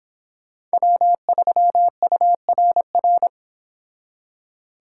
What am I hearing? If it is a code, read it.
Morse code, heard as a keyed tone, about 25 wpm, 700 Hz. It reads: W3URR